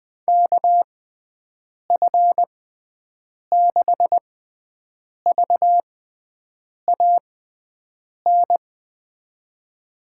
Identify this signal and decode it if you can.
Morse code, heard as a keyed tone, about 21 wpm, spaced as for 7 wpm, 700 Hz. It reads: KF6VAN
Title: USA Call Signs - 6 Characters - 6x Character Spacing - 20wpm